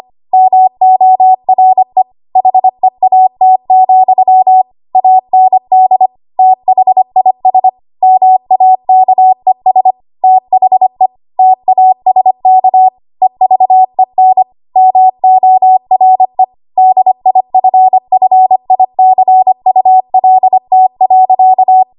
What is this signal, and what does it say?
Morse code, heard as a keyed tone, about 25 wpm, 750 Hz. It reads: MORE HEAT, AND THIS MAKES THE TASK EVEN MORE DIFFICULT.
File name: Archive / 25 WPM / 25WPM